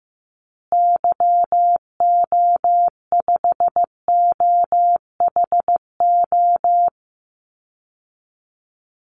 Morse code, heard as a keyed tone, about 15 wpm, 700 Hz. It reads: YO5OHO
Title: Call Signs Repeated 3x - 15wpm